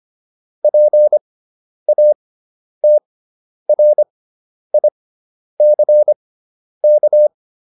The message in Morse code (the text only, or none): PATRICK